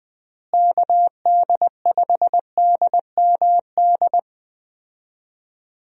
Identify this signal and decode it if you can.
Morse code, heard as a keyed tone, about 20 wpm, 700 Hz. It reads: KD5DMD
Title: USA Call Signs - 6 Characters - Repeated 2x - 20wpm